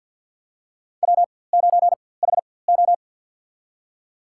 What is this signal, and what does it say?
Morse code, heard as a keyed tone, about 48 wpm, spaced as for 23 wpm, 700 Hz. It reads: W9HO
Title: USA Call Signs - 4x Character Spacing - 50wpm